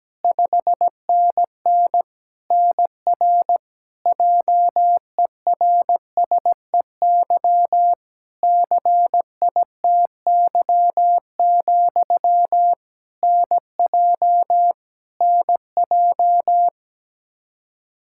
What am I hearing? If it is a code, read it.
Morse code, heard as a keyed tone, about 17 wpm, 700 Hz. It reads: 5NN NR JERSEY CITY, NJ NJ
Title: Practice QSO - Exchange 1 - Beginning - 17wpm